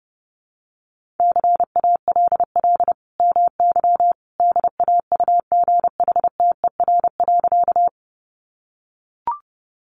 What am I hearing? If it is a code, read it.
Morse code, heard as a keyed tone, about 30 wpm, 700 Hz. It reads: CALL MY DAUGHTER.
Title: Sentences from Top 1500 Words 30wpm